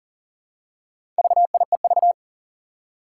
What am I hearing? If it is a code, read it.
Morse code, heard as a keyed tone, about 40 wpm, 700 Hz. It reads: VIEV
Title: Four Challenging Letter-Number Combinations 40wpm